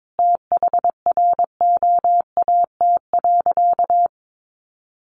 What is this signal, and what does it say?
Morse code, heard as a keyed tone, about 22 wpm, 700 Hz. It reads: THROAT.
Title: A Sound Of Thunder 22wpm